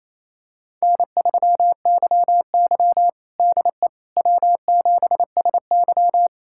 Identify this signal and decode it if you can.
Morse code, heard as a keyed tone, about 28 wpm, 700 Hz. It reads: N3YY DE W7SY
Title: USA Callsign DE Callsign - 4 Characters - 28wpm